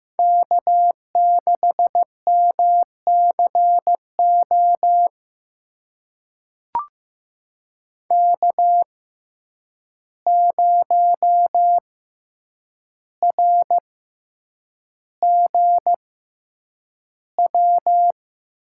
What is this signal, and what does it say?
Morse code, heard as a keyed tone, about 15 wpm, spaced as for 8 wpm, 700 Hz. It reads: K6MCO K0RGW
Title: USA Call Signs - 5 Characters - 6x Character Spacing - 15wpm